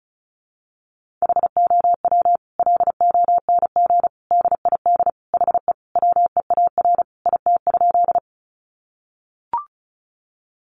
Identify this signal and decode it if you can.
Morse code, heard as a keyed tone, about 34 wpm, 700 Hz. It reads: HOW LONG DID HE WEAR IT?
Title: Sentences from Top 1500 Words 35wpm